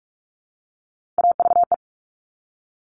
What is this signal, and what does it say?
Morse code, heard as a keyed tone, about 46 wpm, 700 Hz. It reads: AVE
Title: Top Three Letter Combinations in Words 45wpm